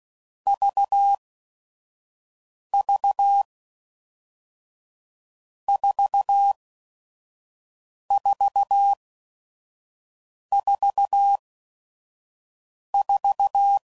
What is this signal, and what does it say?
Morse code, heard as a keyed tone, about 16 wpm, spaced as for 5 wpm, 800 Hz. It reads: VV 4444
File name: Groups / Basic Letter Groups / VB46